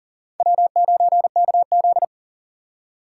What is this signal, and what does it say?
Morse code, heard as a keyed tone, about 55 wpm, spaced as for 48 wpm, 700 Hz. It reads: W9KZ
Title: Speed Racing - USA Call Signs 40wpm